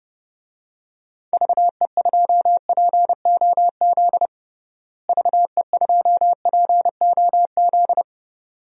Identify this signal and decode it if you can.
Morse code, heard as a keyed tone, about 30 wpm, 700 Hz. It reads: VE2POZ VE2POZ
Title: DXCC-100 Call Signs - 6 Characters - Repeated 2x - 30wpm